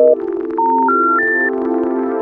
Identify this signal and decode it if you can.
Special information tone, with noise.